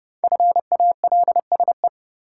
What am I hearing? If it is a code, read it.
Morse code, heard as a keyed tone, about 30 wpm, 700 Hz. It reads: FALSE
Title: Sentences from Top 1600 Words - New Word Drill Alt - 30wpm